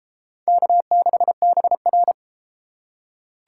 Morse code, heard as a keyed tone, about 31 wpm, 700 Hz. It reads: K6BR